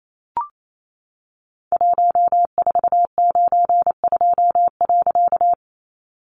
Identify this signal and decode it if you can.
Morse code, heard as a keyed tone, about 28 wpm, 700 Hz. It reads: 1492.
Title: A Sound Of Thunder 28wpm